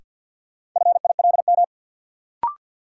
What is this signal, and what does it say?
Morse code, heard as a keyed tone, about 51 wpm, 700 Hz. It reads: FILM
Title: Sentences from Top 1000 Words - Review Part 3 - New Word Drill - 50wpm